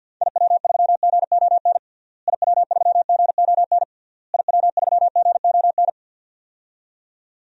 Morse code, heard as a keyed tone, about 46 wpm, 700 Hz. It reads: IW3GON IW3GON IW3GON
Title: DXCC-100 Call Signs - 6 Characters - Repeated 3x - 50wpm